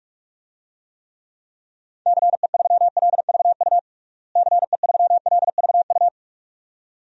Morse code, heard as a keyed tone, about 46 wpm, 700 Hz. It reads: CE3LVU CE3LVU
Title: DXCC-100 Call Signs - 6 Characters - Repeated 2x - 45wpm